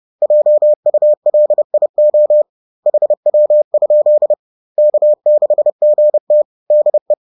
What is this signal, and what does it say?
Morse code, heard as a keyed tone, about 30 wpm, 600 Hz. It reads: JULIO HW? K6GT DE
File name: 30wpm-4